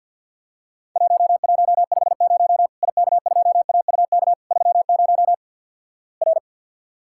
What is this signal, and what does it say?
Morse code, heard as a keyed tone, about 50 wpm, 700 Hz. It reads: 1150 IK2AUK 30 R
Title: Worked All Europe DX Contest - QTC Exchange - 50wpm